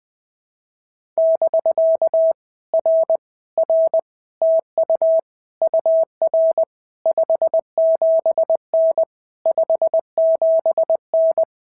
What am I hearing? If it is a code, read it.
Morse code, heard as a keyed tone, about 20 wpm, 650 Hz. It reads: <BK> R R TU UR 57N 57N
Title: POTA QSO Practice - 20wpm